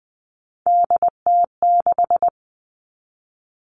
Morse code, heard as a keyed tone, about 20 wpm, 700 Hz. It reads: DT6
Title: All Cut Numbers - 3-Digit 20wpm